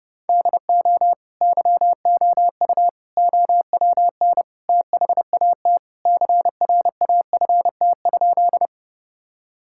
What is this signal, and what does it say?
Morse code, heard as a keyed tone, about 30 wpm, 700 Hz. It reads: DO YOU OWN THAT CRAFT?
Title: Sentences from Top 1000 Words - Review Part 6 - 30wpm